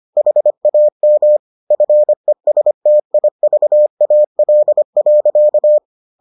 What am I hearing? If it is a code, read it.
Morse code, heard as a keyed tone, about 25 wpm, 600 Hz. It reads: HAM FESTIVAL.